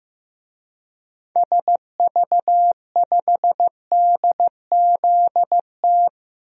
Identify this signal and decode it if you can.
Morse code, heard as a keyed tone, about 16 wpm, 700 Hz. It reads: SV5DZT
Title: DXCC-100 Call Signs - 6 Characters - 15wpm